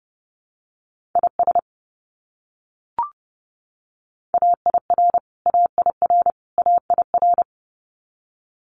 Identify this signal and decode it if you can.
Morse code, heard as a keyed tone, about 30 wpm, 700 Hz. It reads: IS AIR AIR AIR
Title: Top 300 Words Repeated 30wpm